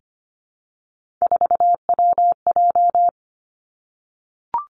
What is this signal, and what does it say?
Morse code, heard as a keyed tone, about 25 wpm, 700 Hz. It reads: 4WJ
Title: Three Challenging Letter-Number Combinations 25wpm